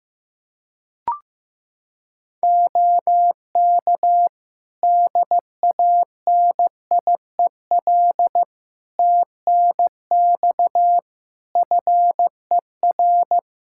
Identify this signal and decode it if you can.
Morse code, heard as a keyed tone, about 15 wpm, 700 Hz. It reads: OK DANIEL TNX FER
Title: K1USN SST QSO Practice 15wpm